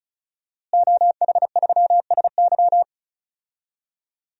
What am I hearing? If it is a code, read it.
Morse code, heard as a keyed tone, about 36 wpm, 700 Hz. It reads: OH3SY
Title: DXCC-100 Call Signs - 5 Characters - Repeated 2x - 35wpm